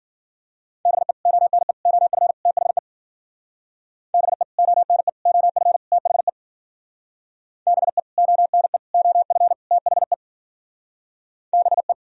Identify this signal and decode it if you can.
Morse code, heard as a keyed tone, about 56 wpm, 700 Hz. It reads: BE ONE OF THE BE ONE OF THE BE ONE OF THE BE